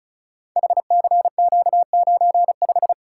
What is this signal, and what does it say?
Morse code, heard as a keyed tone, about 36 wpm, 690 Hz. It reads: HCQ95
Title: Code Groups - 5 Letter:Numbers - 35wpm